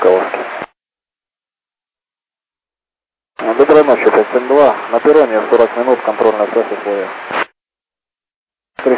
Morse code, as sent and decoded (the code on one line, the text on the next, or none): none